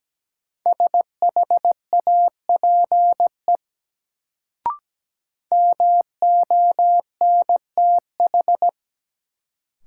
Morse code, heard as a keyed tone, about 17 wpm, 700 Hz. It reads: SHAPE MONTH
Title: Sets of 5 Words Encore - New Word Drill - 17wpm